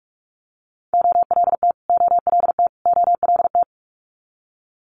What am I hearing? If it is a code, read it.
Morse code, heard as a keyed tone, about 46 wpm, 700 Hz. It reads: OLT OLT OLT
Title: Multi-letter Phonemes Repeated 3x 45wpm